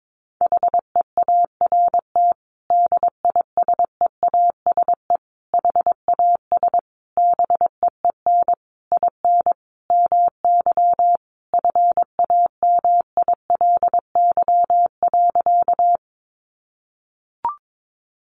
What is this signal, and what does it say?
Morse code, heard as a keyed tone, about 22 wpm, 700 Hz. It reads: HEART DISEASE HAS BEEN IN MY FAMILY.
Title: Sentences from Top 1200 Words 22wpm